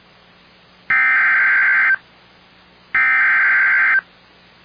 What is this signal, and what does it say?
Emergency Alert System (SAME) tones (header bursts), heard as harsh buzzing bursts. It is heard from 0.9 to 3.9 s.